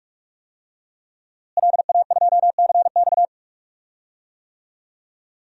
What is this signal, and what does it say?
Morse code, heard as a keyed tone, about 44 wpm, 700 Hz. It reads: LA2CX
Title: DXCC-100 Call Signs - 5 Characters - Repeated 2x - 45wpm